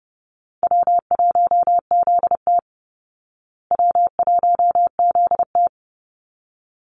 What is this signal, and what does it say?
Morse code, heard as a keyed tone, about 30 wpm, 700 Hz. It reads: W1ZT W1ZT
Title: Easy Call Signs Repeated 2x - 30wpm